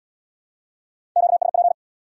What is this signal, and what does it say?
Morse code, heard as a keyed tone, about 62 wpm, 700 Hz. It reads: QSP
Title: Speed-Racing Q-Codes - 50wpm